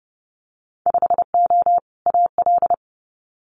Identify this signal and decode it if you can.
Morse code, heard as a keyed tone, about 30 wpm, 700 Hz. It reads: 5O AL
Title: Winter Field Day Exchanges 30wpm